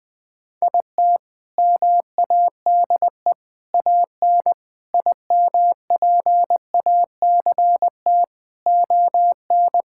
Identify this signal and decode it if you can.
Morse code, heard as a keyed tone, about 21 wpm, 700 Hz. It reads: IT MADE AN IMPACT ON
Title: Sentences from Top 1000 Words - Review Part 6 - 20wpm